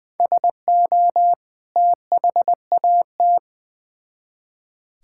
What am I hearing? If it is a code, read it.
Morse code, heard as a keyed tone, about 20 wpm, 700 Hz. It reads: SO THAT
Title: Build A Phrase - 2-Word Phrases - 20wpm